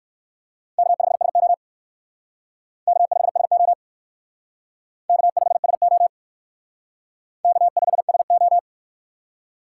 Morse code, heard as a keyed tone, about 56 wpm, 700 Hz. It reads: K5SO K5SO K5SO K5SO
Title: Speed Racing - USA 1x2 Call Signs 45wpm